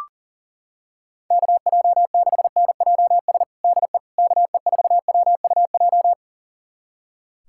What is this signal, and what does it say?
Morse code, heard as a keyed tone, about 40 wpm, 700 Hz. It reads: KJ6NJS DE KE4WUJ
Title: USA Callsign DE Callsign 40wpm